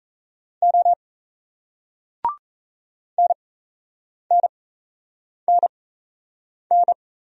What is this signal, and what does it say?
Morse code, heard as a keyed tone, about 36 wpm, spaced as for 8 wpm, 700 Hz. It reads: O NNNN